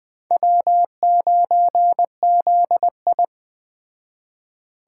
Morse code, heard as a keyed tone, about 21 wpm, 700 Hz. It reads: W9ZI
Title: Speed Racing - USA 1x2 Call Signs 17wpm